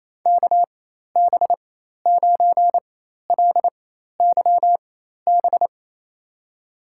Morse code, heard as a keyed tone, about 28 wpm, spaced as for 13 wpm, 700 Hz. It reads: KB9LYB
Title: USA Call Signs - 4x Character Spacing - 28wpm